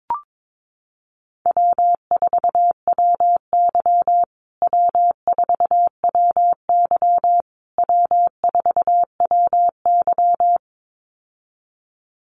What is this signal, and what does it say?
Morse code, heard as a keyed tone, about 22 wpm, 700 Hz. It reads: W4WY W4WY W4WY
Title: USA Call Signs Repeated 3x 22wpm